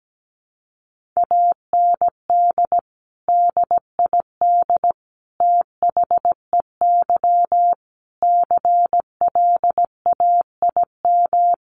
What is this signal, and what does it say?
Morse code, heard as a keyed tone, about 17 wpm, 700 Hz. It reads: AND DID THEY CLAIM
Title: Sentences from Top 1500 Words 17wpm